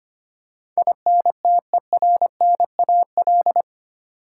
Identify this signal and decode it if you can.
Morse code, heard as a keyed tone, about 25 wpm, 690 Hz. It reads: INTERNAL
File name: 8 Letter Words - First 3 Letters Given - 25wpm